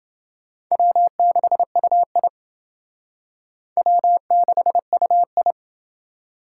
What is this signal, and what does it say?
Morse code, heard as a keyed tone, about 28 wpm, 700 Hz. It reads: W6UI W6UI